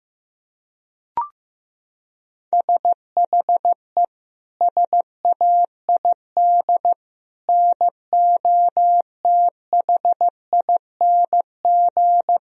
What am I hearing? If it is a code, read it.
Morse code, heard as a keyed tone, about 15 wpm, 700 Hz. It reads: SHE SAID NOTHING
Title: Sentences from Top 1000 Words - Review Part 2 - 15wpm